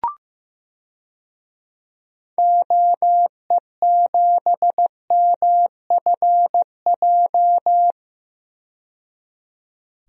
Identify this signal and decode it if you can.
Morse code, heard as a keyed tone, about 15 wpm, 700 Hz. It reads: OE7MFJ
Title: DXCC-100 Call Signs - 6 Characters - 15wpm